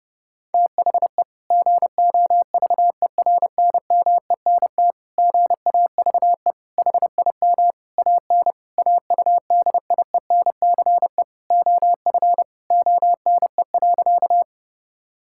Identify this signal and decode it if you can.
Morse code, heard as a keyed tone, about 30 wpm, 700 Hz. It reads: THE GOVERNMENT GAVE HIM AN AUDIENCE OF ONE.
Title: Sentences from Top 1000 Words - Review Part 5 - 30wpm